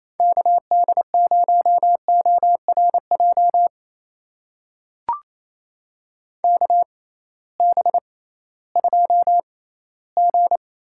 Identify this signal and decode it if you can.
Morse code, heard as a keyed tone, about 28 wpm, spaced as for 16 wpm, 700 Hz. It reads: KD0ORJ KB2G
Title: USA Call Signs - 6 Characters - 6x Character Spacing - 28wpm